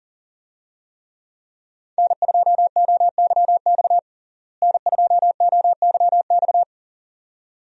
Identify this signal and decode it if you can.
Morse code, heard as a keyed tone, about 40 wpm, 700 Hz. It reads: N2OYX N2OYX
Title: USA Call Signs - 5 Characters - Repeated 2x - 40wpm